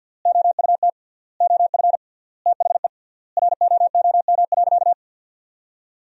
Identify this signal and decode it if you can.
Morse code, heard as a keyed tone, about 52 wpm, 700 Hz. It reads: OUT OF THE ROOM.
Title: 3x Word-Spacing - Sentences from Top 500 Words - 50wpm